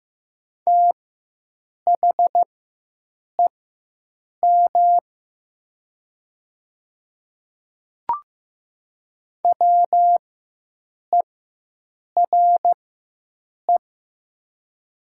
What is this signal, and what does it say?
Morse code, heard as a keyed tone, about 15 wpm, spaced as for 7 wpm, 700 Hz. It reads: THEM WERE
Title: 4x Character Spacing - Top 100 Words - 15wpm